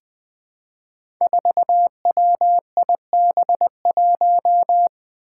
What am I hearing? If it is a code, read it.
Morse code, heard as a keyed tone, about 20 wpm, 700 Hz. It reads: 4WIB1
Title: Five Challenging Letter-Number Combinations 20wpm